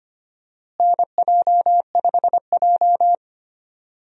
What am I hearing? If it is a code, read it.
Morse code, heard as a keyed tone, about 25 wpm, 700 Hz. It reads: NJ5J
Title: USA Call Signs - 4 Characters - Repeated 3x - 25wpm